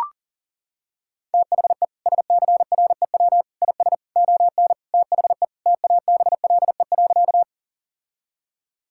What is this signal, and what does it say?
Morse code, heard as a keyed tone, about 37 wpm, 700 Hz. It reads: THE SCREW IS ON THE TABLE.
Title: Sentences from Top 1600 Words 40wpm